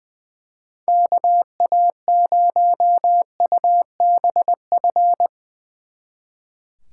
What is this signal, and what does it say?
Morse code, heard as a keyed tone, about 23 wpm, 700 Hz. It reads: KA0UBF
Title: Speed Racing - USA 2x3 Call Signs 20wpm